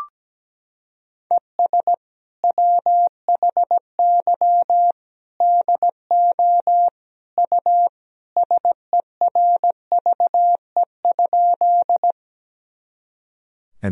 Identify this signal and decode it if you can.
Morse code, heard as a keyed tone, about 17 wpm, 700 Hz. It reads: ES WHY DO U SERVE?
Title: Sentences from Top 400 Words and Abbreviations 17wpm